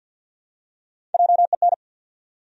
Morse code, heard as a keyed tone, about 49 wpm, 700 Hz. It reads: JEN
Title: USA Common Names - Max 4 Letters - 50wpm